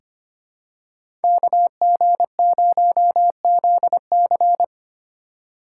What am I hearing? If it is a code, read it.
Morse code, heard as a keyed tone, about 25 wpm, 700 Hz. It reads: KG0ZC
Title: USA Call Signs - 5 Characters - 2x Character Spacing - 25wpm